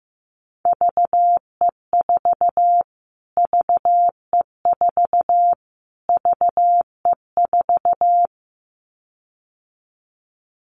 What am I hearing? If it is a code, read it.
Morse code, heard as a keyed tone, about 15 wpm, 700 Hz. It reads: VE4 VE4 VE4
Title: Callsign Prefixes for Top 25 DXCC - Repeated 3x - 15wpm